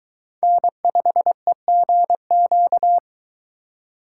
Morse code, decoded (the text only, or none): N5EGQ